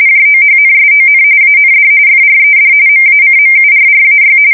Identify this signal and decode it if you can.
RTTY, about 45.45 baud. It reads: JOHNSON AND FRED MACMURRAY
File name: fsk170 / battleshipnj